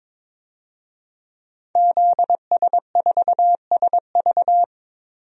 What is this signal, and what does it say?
Morse code, heard as a keyed tone, about 21 wpm, 700 Hz. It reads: ZS4SV
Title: DXCC-100 Call Signs - 5 Characters - 22wpm